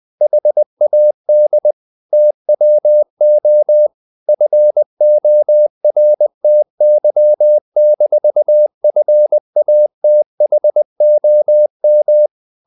Morse code, heard as a keyed tone, about 20 wpm, 600 Hz. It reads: HAD TWO FORTY-FATHOM
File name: the old man and the sea / the old man and the sea part 1 20wpm